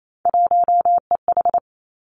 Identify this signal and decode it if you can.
Morse code, heard as a keyed tone, about 28 wpm, 700 Hz. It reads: 1EH
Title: Three Challenging Letter-Number Combinations 28wpm